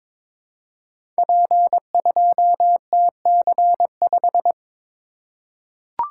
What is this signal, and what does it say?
Morse code, heard as a keyed tone, about 22 wpm, 700 Hz. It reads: P2TC5
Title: Code Groups - 5 Letter:Numbers - 22wpm